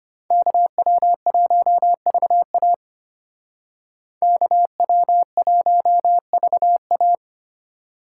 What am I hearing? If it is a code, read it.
Morse code, heard as a keyed tone, about 33 wpm, 700 Hz. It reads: KW1VA KW1VA